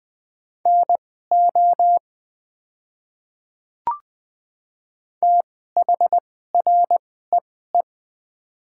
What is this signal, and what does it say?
Morse code, heard as a keyed tone, about 20 wpm, spaced as for 14 wpm, 700 Hz. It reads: NO THREE
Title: 2x Character Spacing - Top 200 Words - 20wpm